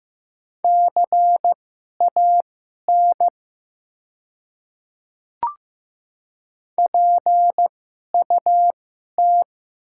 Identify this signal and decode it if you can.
Morse code, heard as a keyed tone, about 15 wpm, spaced as for 11 wpm, 700 Hz. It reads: CAN PUT
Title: 2x Character Spacing - Top 200 Words - 15wpm